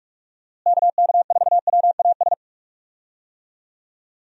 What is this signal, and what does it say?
Morse code, heard as a keyed tone, about 46 wpm, 700 Hz. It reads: KK4WAS
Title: USA Call Signs - 6 Characters - Repeated 3x - 45wpm